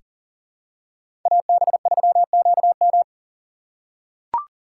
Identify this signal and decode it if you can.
Morse code, heard as a keyed tone, about 40 wpm, 700 Hz. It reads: AB3QM
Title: USA Call Signs - 5 Characters - 6x Character Spacing - 40wpm